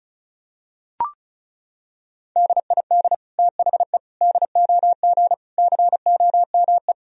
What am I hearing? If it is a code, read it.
Morse code, heard as a keyed tone, about 34 wpm, 700 Hz. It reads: DID THE DOG COME
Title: Sentences from Top 1000 Words - Review Part 2 - 35wpm